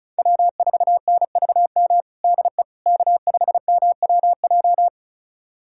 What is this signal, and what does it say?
Morse code, heard as a keyed tone, about 36 wpm, 700 Hz. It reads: W4NVM DE K5MWJ
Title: USA Callsign DE Callsign - 5 Characters - 35wpm